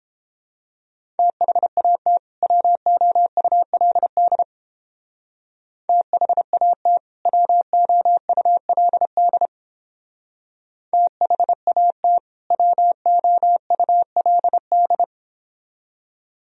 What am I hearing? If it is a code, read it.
Morse code, heard as a keyed tone, about 30 wpm, 700 Hz. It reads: THAT WOULD THAT WOULD THAT WOULD